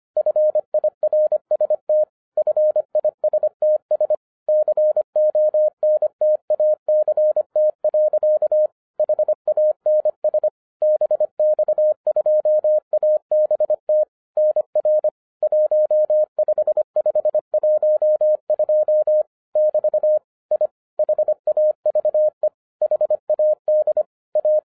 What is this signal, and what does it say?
Morse code, heard as a keyed tone, about 25 wpm, 610 Hz. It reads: FIRST FISTS CONTACT. HANS BX2ABT NR 15512 <BT> I HAVE HAD A
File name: 2012-01-mp3 / n06-FIRST FISTS CONTACT-25wpm